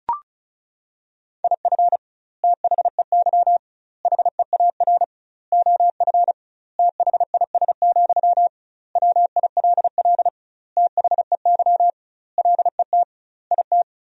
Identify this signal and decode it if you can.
Morse code, heard as a keyed tone, about 34 wpm, 700 Hz. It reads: IF THEY HEAR OF THIS, WILL THEY LET IT
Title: 2x Word-Spacing - Sentences from Top 400 Words - 35wpm